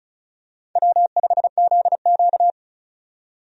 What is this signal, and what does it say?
Morse code, heard as a keyed tone, about 36 wpm, 700 Hz. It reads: W5ZQ
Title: USA Call Signs - 4 Characters - 35wpm